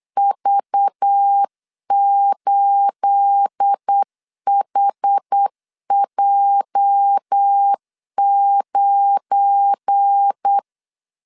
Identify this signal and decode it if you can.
Morse code, heard as a keyed tone, about 8 wpm, 800 Hz. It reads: V8HJ9
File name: LEZIONI / 9 28 8wpm 3512